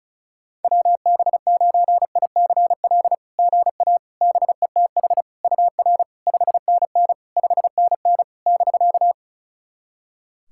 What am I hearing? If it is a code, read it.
Morse code, heard as a keyed tone, about 34 wpm, 690 Hz. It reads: WB9ICL GA BETH UR 5NN 5NN <BK>
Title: SOTA QSO Practice 35wpm